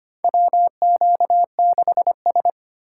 Morse code, heard as a keyed tone, about 25 wpm, 700 Hz. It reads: WQ6S
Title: USA Call Signs - 3x Character Spacing - 25wpm